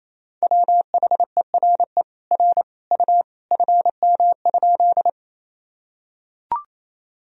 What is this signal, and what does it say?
Morse code, heard as a keyed tone, about 28 wpm, 700 Hz. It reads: WHERE R U FM?
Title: Sentences from Top 200 Words and Abbreviations 28wpm